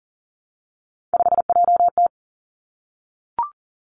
Morse code, heard as a keyed tone, about 38 wpm, 700 Hz. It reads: 5JT
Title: Three Challenging Letter-Number Combinations 40wpm